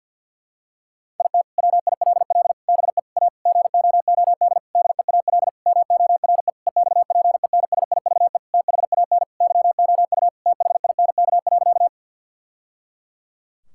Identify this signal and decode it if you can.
Morse code, heard as a keyed tone, about 50 wpm, 700 Hz. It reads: IT WILL BE A GOOD DEAL MORE EXPENSIVE THAN YOU THINK.
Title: Sentences from Top 1000 Words - Review Part 6 - 50wpm